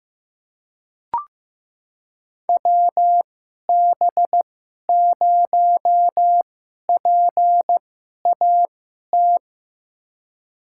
Morse code, heard as a keyed tone, about 15 wpm, spaced as for 11 wpm, 700 Hz. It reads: WB0PAT
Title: USA Call Signs - 6 Characters - 2x Character Spacing - 15wpm